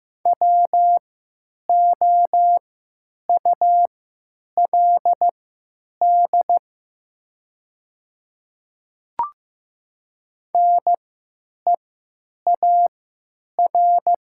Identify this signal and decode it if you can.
Morse code, heard as a keyed tone, about 15 wpm, spaced as for 9 wpm, 700 Hz. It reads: WOULD NEAR
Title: 3x Character Spacing - Top 500 Words - 15wpm